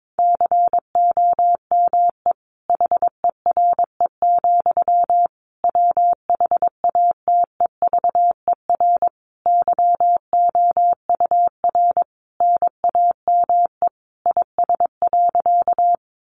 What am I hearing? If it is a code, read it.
Morse code, heard as a keyed tone, about 22 wpm, 700 Hz. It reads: COME HERE, WHATEVER YOUR NAME IS.
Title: Sentences from Top 1100 Words 22wpm